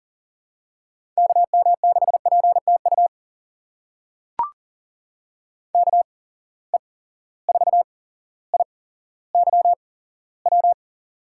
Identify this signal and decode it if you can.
Morse code, heard as a keyed tone, about 39 wpm, spaced as for 17 wpm, 700 Hz. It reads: KM6PTU KE4IYW
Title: USA Call Signs - 6 Characters - 8x Character Spacing - 40wpm